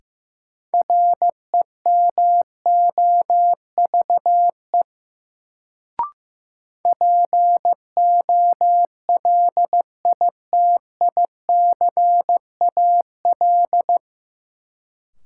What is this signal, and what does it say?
Morse code, heard as a keyed tone, about 16 wpm, 700 Hz. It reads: REMOVE POLITICAL
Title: Sentences from Top 800 Words - New Word Drill - 15wpm